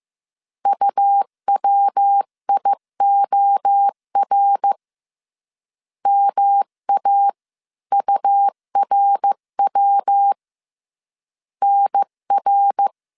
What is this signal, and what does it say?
Morse code, heard as a keyed tone, about 15 wpm, 800 Hz. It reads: UWIOR MA URW NR
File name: CW / DET / 1 3 1500C S C 14wpm 2458